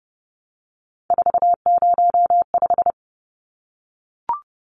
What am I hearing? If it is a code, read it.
Morse code, heard as a keyed tone, about 30 wpm, 700 Hz. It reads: 405
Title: Numbers - 3-Digit 30wpm